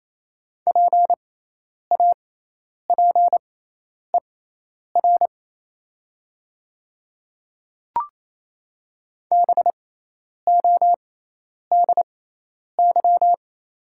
Morse code, heard as a keyed tone, about 28 wpm, spaced as for 10 wpm, 690 Hz. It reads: PAPER BODY
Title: 6x Character Spacing - Top 500 Words - 28wpm